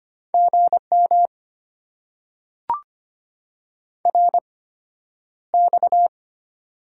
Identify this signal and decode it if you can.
Morse code, heard as a keyed tone, about 25 wpm, spaced as for 11 wpm, 700 Hz. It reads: GM RX
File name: Top 90 QSO Elements - 8x Character Spacing - 25wpm